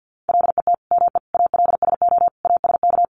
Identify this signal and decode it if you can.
Morse code, heard as a keyed tone, about 50 wpm, 700 Hz. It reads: LET ME ALSO ASK
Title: Sentences from Top 400 Words 50wpm